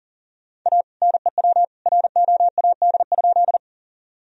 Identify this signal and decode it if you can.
Morse code, heard as a keyed tone, about 37 wpm, 700 Hz. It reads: A NEW ROAD?
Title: Sentences from Top 1000 Words - Review Part 2 - 40wpm